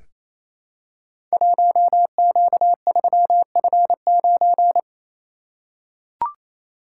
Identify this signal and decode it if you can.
Morse code, heard as a keyed tone, about 29 wpm, 700 Hz. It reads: 1Q3F9
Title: Code Groups - 5 Letter:Numbers - 28wpm